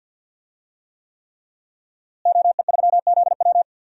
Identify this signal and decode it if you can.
Morse code, heard as a keyed tone, about 51 wpm, 700 Hz. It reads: OE3ZW
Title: DXCC-100 Call Signs - 5 Characters - Repeated 3x - 50wpm